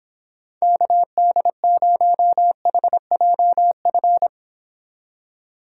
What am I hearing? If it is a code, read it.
Morse code, heard as a keyed tone, about 29 wpm, 700 Hz. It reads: KD0HJF